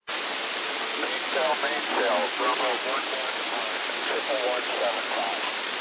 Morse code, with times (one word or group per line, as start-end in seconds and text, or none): none